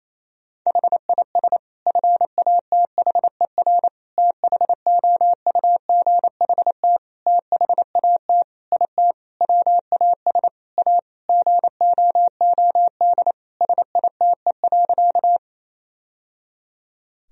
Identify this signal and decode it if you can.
Morse code, heard as a keyed tone, about 28 wpm, 700 Hz. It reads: HIS FATHER THOUGHT THAT IT WAS A GOOD SITE.
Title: Sentences from Top 1000 Words - Review Part 4 - 28wpm